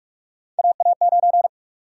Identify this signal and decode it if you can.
Morse code, heard as a keyed tone, about 42 wpm, 700 Hz. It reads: AA9
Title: USA Call Sign Prefixes - 45wpm